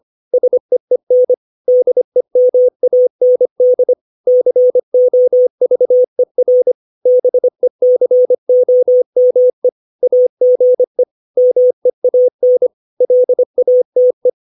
Morse code, heard as a keyed tone, about 25 wpm, 500 Hz. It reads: SEEN DEMAND COVER BECOME AGE MEAN LATE
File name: 632 words / 632 Words 25WPM